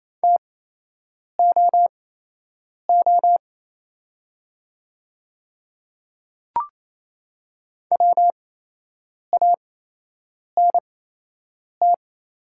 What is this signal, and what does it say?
Morse code, heard as a keyed tone, about 28 wpm, spaced as for 8 wpm, 700 Hz. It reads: TOO WANT
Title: 8x Character Spacing - Top 300 Words - 28wpm